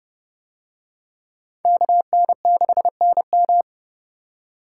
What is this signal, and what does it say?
Morse code, heard as a keyed tone, about 27 wpm, 700 Hz. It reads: KN6NM